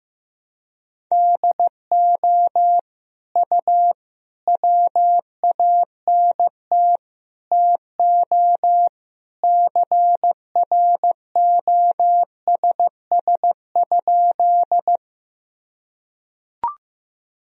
Morse code, heard as a keyed tone, about 15 wpm, 700 Hz. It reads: DO U WANT TO CROSS?
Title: Sentences from Top 300 Words and Abbreviations 15wpm